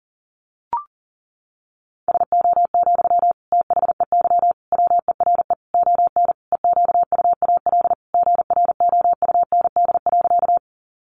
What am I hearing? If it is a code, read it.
Morse code, heard as a keyed tone, about 40 wpm, 700 Hz. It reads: SO, THEY WERE ON EQUAL GROUND.
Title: Sentences from Top 1400 Words 40wpm